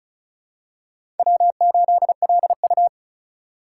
Morse code, heard as a keyed tone, about 34 wpm, 700 Hz. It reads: W8LU
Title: USA Call Signs - 4 Characters - 6x Character Spacing - 35wpm